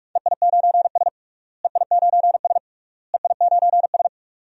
Morse code, heard as a keyed tone, about 41 wpm, 700 Hz. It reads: EI9S EI9S EI9S
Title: DXCC-100 Call Signs - 4 Characters - Repeated 3x - 45wpm